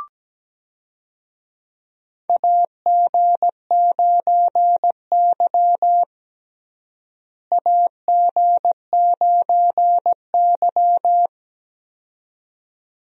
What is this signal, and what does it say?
Morse code, heard as a keyed tone, about 17 wpm, 700 Hz. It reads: AG9Y AG9Y
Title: USA Call Signs - 4 Characters - Repeated 2x - 17wpm